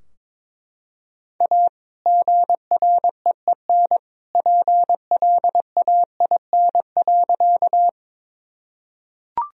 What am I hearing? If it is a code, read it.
Morse code, heard as a keyed tone, about 22 wpm, 700 Hz. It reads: A GREEN PLAIN.
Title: Sentences from Top 1000 Words - Review Part 2 - 22wpm